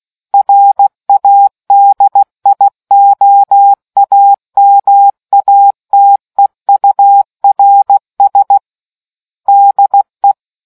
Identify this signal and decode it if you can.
Morse code, heard as a keyed tone, about 16 wpm, 800 Hz. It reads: RADIOAMATEURS DE